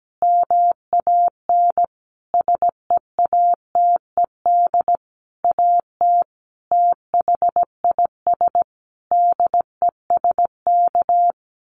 Morse code, heard as a keyed tone, about 17 wpm, 700 Hz. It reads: MAN SEATED AT THIS DESK
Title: A Sound Of Thunder 17wpm